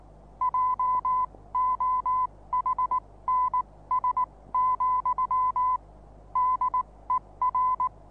Morse code, heard as a keyed tone, about 19 wpm, spaced as for 16 wpm, 1010 Hz. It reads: JOHNS, DER
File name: repos / Morsekurs / DJ4UF Morsen26